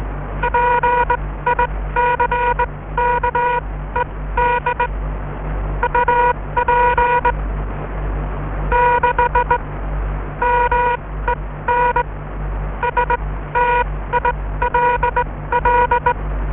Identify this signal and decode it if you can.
Morse code, heard as a keyed tone, about 21 wpm, spaced as for 16 wpm, 960 Hz. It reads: PICKED UP 6 MEN STILL